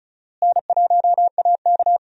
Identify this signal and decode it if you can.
Morse code, heard as a keyed tone, about 36 wpm, 700 Hz. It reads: N1AK
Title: USA Call Signs - 4 Characters - Repeated 3x - 35wpm